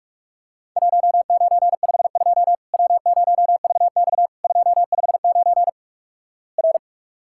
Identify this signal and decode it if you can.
Morse code, heard as a keyed tone, about 46 wpm, 700 Hz. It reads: 1952 W0VX 259 R
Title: Worked All Europe DX Contest - QTC Exchange - 45wpm